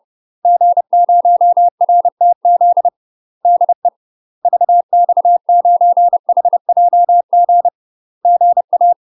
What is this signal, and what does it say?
Morse code, heard as a keyed tone, about 30 wpm, 700 Hz. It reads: G0RTZ DE VX9HJG GA
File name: UK MASTER QSOs TRAINING / 30WPM / UK MASTER QSOs 0030